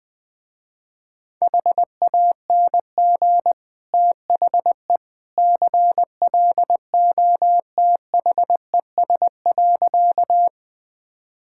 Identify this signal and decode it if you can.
Morse code, heard as a keyed tone, about 20 wpm, 700 Hz. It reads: HANG THE CLOTHES.
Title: Easy Sentences with Hints - 20wpm